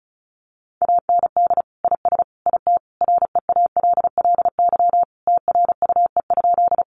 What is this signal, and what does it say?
Morse code, heard as a keyed tone, about 34 wpm, 700 Hz. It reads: AND IS IT REALLY TRUE?
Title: Sentences from Top 600 Words 35wpm